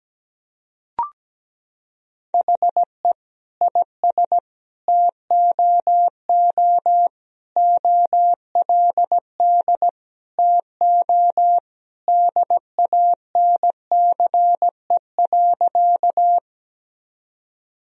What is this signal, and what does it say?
Morse code, heard as a keyed tone, about 17 wpm, 700 Hz. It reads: HE IS TOO OLD TO DANCE.
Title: Sentences from Top 1000 Words - Review Part 4 - 17wpm